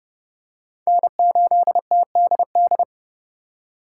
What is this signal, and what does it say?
Morse code, heard as a keyed tone, about 30 wpm, 700 Hz. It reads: N8TDD